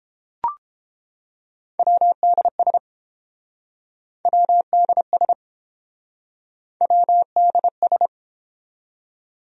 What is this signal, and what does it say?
Morse code, heard as a keyed tone, about 30 wpm, 700 Hz. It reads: WDS WDS WDS